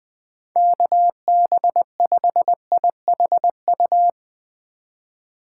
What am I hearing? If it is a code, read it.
Morse code, heard as a keyed tone, about 20 wpm, 700 Hz. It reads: KB5IHU
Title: Speed Racing - USA Call Signs 17wpm